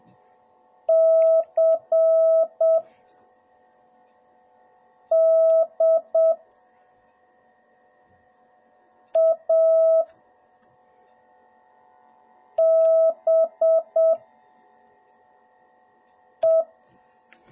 Morse code, heard as a keyed tone, about 7 wpm, spaced as for 3 wpm, 650 Hz. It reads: CDABE